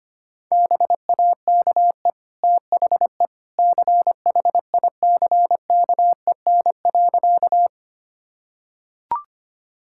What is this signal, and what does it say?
Morse code, heard as a keyed tone, about 25 wpm, 700 Hz. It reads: BAKE THE CHICKEN.
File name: Easy Sentences with Hints - 25wpm